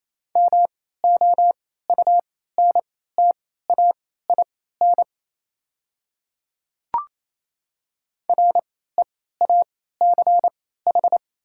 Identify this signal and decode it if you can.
Morse code, heard as a keyed tone, about 28 wpm, spaced as for 16 wpm, 690 Hz. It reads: MOUNTAIN REACH
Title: 3x Character Spacing - Top 500 Words - 28wpm